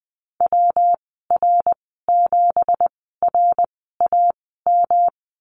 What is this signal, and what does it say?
Morse code, heard as a keyed tone, about 20 wpm, spaced as for 15 wpm, 700 Hz. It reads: WR7RAM
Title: 186 Lesson 30g - Letter b - Call Signs 20wpm